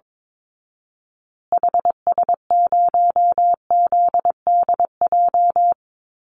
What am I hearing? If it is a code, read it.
Morse code, heard as a keyed tone, about 22 wpm, 700 Hz. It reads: HS0ZDJ
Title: Call Signs Repeated 3x - 22wpm